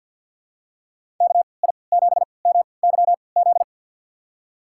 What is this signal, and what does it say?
Morse code, heard as a keyed tone, about 52 wpm, spaced as for 29 wpm, 700 Hz. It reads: KI7MYZ
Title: USA Call Signs - 3x Character Spacing - 50wpm